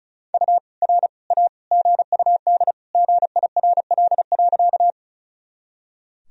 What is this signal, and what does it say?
Morse code, heard as a keyed tone, about 36 wpm, 700 Hz. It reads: U R A GUD GIRL.
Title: Sentences from Top 400 Words and Abbreviations 35wpm